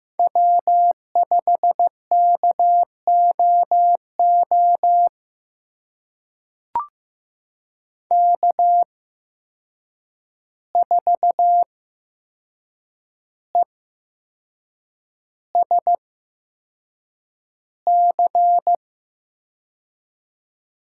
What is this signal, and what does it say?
Morse code, heard as a keyed tone, about 15 wpm, spaced as for 6 wpm, 700 Hz. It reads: W5KOO K4ESC
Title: USA Call Signs - 5 Characters - 8x Character Spacing - 15wpm